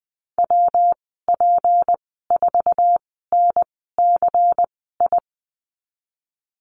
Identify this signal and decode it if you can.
Morse code, heard as a keyed tone, about 20 wpm, spaced as for 15 wpm, 700 Hz. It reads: WP4NCI
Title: 179 Lesson 29g - Slash - Call Signs 20wpm